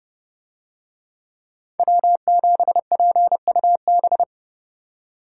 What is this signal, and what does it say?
Morse code, heard as a keyed tone, about 29 wpm, 700 Hz. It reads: W7PUB